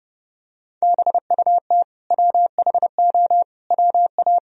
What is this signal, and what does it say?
Morse code, heard as a keyed tone, about 32 wpm, 700 Hz. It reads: BUT WHO WA
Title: Sentences from Top 200 Words and Abbreviations 30wpm